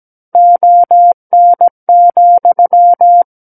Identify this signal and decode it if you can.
Morse code, heard as a keyed tone, about 17 wpm, 700 Hz. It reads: ON,
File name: Book One / 08-Chapter 8 - 17 WPM